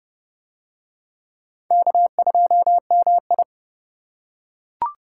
Morse code, heard as a keyed tone, about 29 wpm, 700 Hz. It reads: K2MI